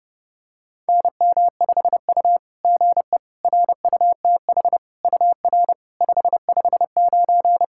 Morse code, heard as a keyed tone, about 30 wpm, 690 Hz. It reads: NM5U GE RUTH UR 559
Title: SOTA QSO Practice 30wpm